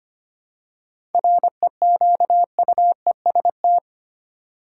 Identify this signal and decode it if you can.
Morse code, heard as a keyed tone, about 25 wpm, 700 Hz. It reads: REQUEST
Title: Seven Letter Words - 25wpm